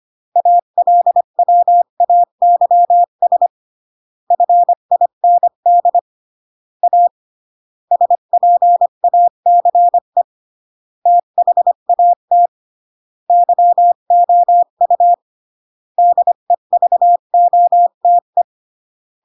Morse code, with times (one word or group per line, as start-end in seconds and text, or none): 0.4-3.5 ALWAYS
4.3-6.0 FIND
6.8-7.1 A
7.9-10.2 SPACE
11.1-12.5 THAT
13.3-15.1 YOU
16.0-18.4 DEVOTE